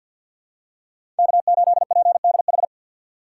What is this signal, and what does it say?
Morse code, heard as a keyed tone, about 52 wpm, 700 Hz. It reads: K8PDH
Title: USA Call Signs - 5 Characters - 2x Character Spacing - 50wpm